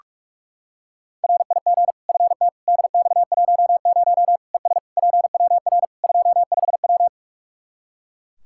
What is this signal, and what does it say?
Morse code, heard as a keyed tone, about 44 wpm, 700 Hz. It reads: RIG FT DX10 ES PWR 25W
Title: Practice QSO - Exchange 2 - Info - 45wpm